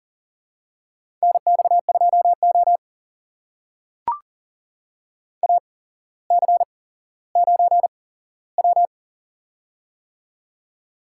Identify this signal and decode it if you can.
Morse code, heard as a keyed tone, about 39 wpm, spaced as for 17 wpm, 700 Hz. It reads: NX2O AC9W
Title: USA Call Signs - 4 Characters - 8x Character Spacing - 40wpm